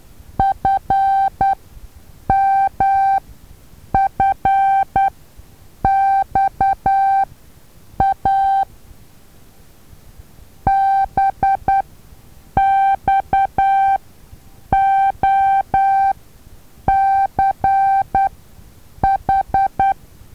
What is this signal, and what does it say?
Morse code, heard as a keyed tone, about 10 wpm, spaced as for 7 wpm, 780 Hz. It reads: FMFXA BXOCH
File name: Taviro HA6VA B